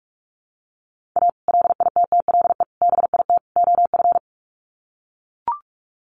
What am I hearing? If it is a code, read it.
Morse code, heard as a keyed tone, about 44 wpm, 700 Hz. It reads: A LITTLE BIT OF
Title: Sets of 4 Words 45wpm